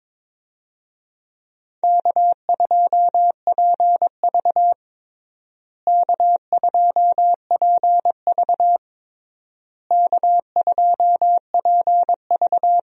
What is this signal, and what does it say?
Morse code, heard as a keyed tone, about 21 wpm, 700 Hz. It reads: K2PV K2PV K2PV
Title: USA Call Signs - 4 Characters - Repeated 3x - 22wpm